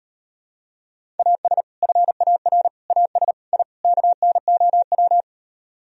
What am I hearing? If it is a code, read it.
Morse code, heard as a keyed tone, about 38 wpm, 700 Hz. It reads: AS FAR AS I KNOW